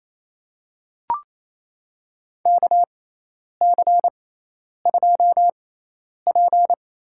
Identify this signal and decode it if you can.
Morse code, heard as a keyed tone, about 28 wpm, spaced as for 10 wpm, 700 Hz. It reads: KC2P
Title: USA Call Signs - 6 Characters - 6x Character Spacing - 28wpm